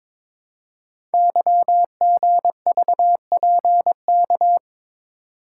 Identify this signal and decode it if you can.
Morse code, heard as a keyed tone, about 22 wpm, 700 Hz. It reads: YGVPK
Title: Code Groups - 5 Letters - 22wpm